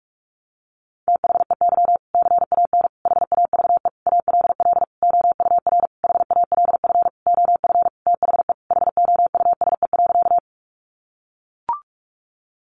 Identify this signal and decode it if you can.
Morse code, heard as a keyed tone, about 44 wpm, 700 Hz. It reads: THEY CAN HAVE ALL OUR HALF OF THE HOUSE.
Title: Sentences from Top 400 Words 45wpm